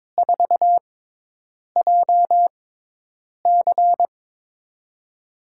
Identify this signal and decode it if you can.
Morse code, heard as a keyed tone, about 22 wpm, 700 Hz. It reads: 4 J C
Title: USA Call Signs - 6x Character Spacing - 22wpm